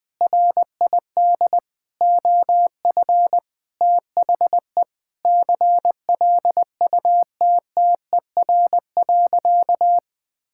Morse code, heard as a keyed tone, about 21 wpm, 700 Hz. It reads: RID OF THE CLUTTER.
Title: Sentences with 5-Word Phrases - Spoken Beginning - 20wpm